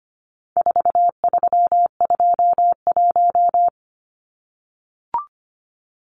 Morse code, heard as a keyed tone, about 25 wpm, 700 Hz. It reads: 4321
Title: Numbers - 4-Digit 25wpm